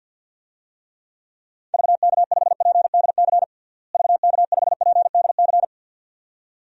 Morse code, heard as a keyed tone, about 51 wpm, 700 Hz. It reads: VK5PDC VK5PDC
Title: DXCC-100 Call Signs - 6 Characters - Repeated 2x - 50wpm